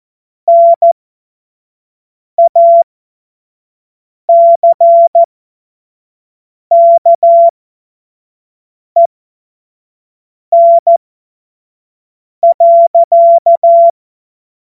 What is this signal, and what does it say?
Morse code, heard as a keyed tone, about 14 wpm, spaced as for 5 wpm, 680 Hz. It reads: NACKEN.
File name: Morsekurs/Witze/Witz 09 Januar 2024 25 F70